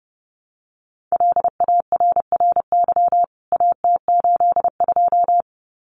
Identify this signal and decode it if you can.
Morse code, heard as a keyed tone, about 30 wpm, 700 Hz. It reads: LARRY AT82
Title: CWT Exchange 30wpm